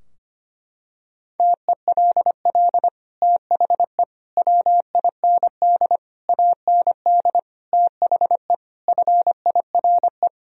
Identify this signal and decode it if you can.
Morse code, heard as a keyed tone, about 25 wpm, 700 Hz. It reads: TELL THE WIND AND THE FIRE
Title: Sentences from Top 1000 Words - Review Part 2 - 25wpm